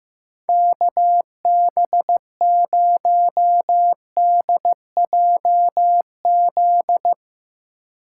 Morse code, heard as a keyed tone, about 15 wpm, 700 Hz. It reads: KB0DJZ
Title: USA Call Signs - 6 Characters - 2x Character Spacing - 15wpm